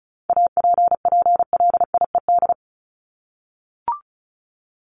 Morse code, heard as a keyed tone, about 34 wpm, 700 Hz. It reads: APPLIED
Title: ED Suffix - Top 100 Words with Base Words 35wpm